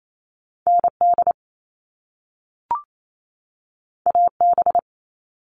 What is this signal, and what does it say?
Morse code, heard as a keyed tone, about 28 wpm, 700 Hz. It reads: ND AB
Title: All Cut Numbers - 2-Digit 28wpm